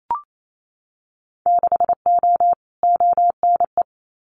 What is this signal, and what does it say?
Morse code, heard as a keyed tone, about 28 wpm, 700 Hz. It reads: 6O ONE
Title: Winter Field Day Exchanges Encore 28wpm